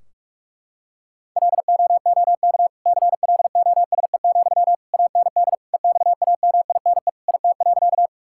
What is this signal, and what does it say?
Morse code, heard as a keyed tone, about 46 wpm, 700 Hz. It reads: LOOK CLOSE, AND EXAMINE IT.
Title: Sentences from Top 1600 Words 45wpm